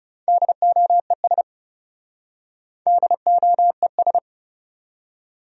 Morse code, heard as a keyed tone, about 37 wpm, 700 Hz. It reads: DOES DOES
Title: Speed-Racing Top 500 Words 30wpm